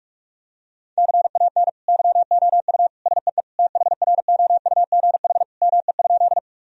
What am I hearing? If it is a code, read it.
Morse code, heard as a keyed tone, about 41 wpm, 700 Hz. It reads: CAN YOU SEE THROUGH ME?
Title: Sentences from Top 1000 Words - Review Part 1 - 45wpm